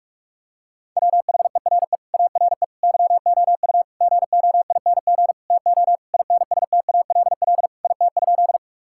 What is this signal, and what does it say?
Morse code, heard as a keyed tone, about 42 wpm, 700 Hz. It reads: WHERE ARE YOU GOING TO INSTALL IT?
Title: Sentences from Top 1600 Words 45wpm